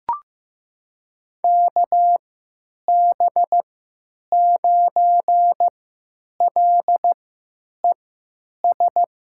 Morse code, heard as a keyed tone, about 15 wpm, spaced as for 9 wpm, 700 Hz. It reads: KB9LES
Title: USA Call Signs - 3x Character Spacing - 15wpm